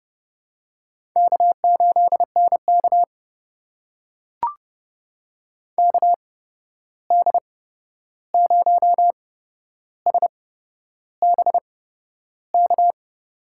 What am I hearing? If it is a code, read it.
Morse code, heard as a keyed tone, about 30 wpm, spaced as for 11 wpm, 700 Hz. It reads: K8NK KD0SBK
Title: USA Call Signs - 8x Character Spacing - 30wpm